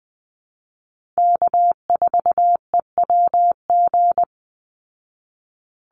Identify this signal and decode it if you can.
Morse code, heard as a keyed tone, about 20 wpm, 700 Hz. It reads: K4EWG
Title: Call Signs Repeated 3x - 20wpm